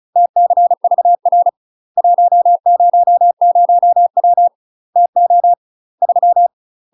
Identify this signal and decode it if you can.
Morse code, heard as a keyed tone, about 34 wpm, 700 Hz. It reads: TCVR 100W TO 3
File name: UK MASTER QSOs TRAINING/35WPM